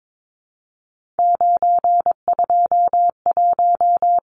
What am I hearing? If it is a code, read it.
Morse code, heard as a keyed tone, about 22 wpm, 700 Hz. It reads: 921
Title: Numbers - 3-Digit 22wpm